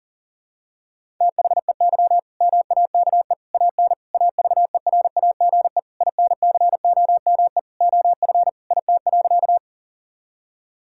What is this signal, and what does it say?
Morse code, heard as a keyed tone, about 40 wpm, 690 Hz. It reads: THEY MAKE AN AVERAGE INCOME OF IT.
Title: Sentences from Top 1000 Words - Review Part 6 - 40wpm